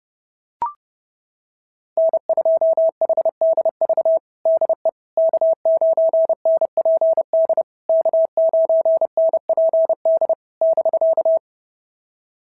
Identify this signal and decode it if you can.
Morse code, heard as a keyed tone, about 30 wpm, 650 Hz. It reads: N2HDV DE K9NPD K9NPD <BK>
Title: SOTA QSO Practice 30wpm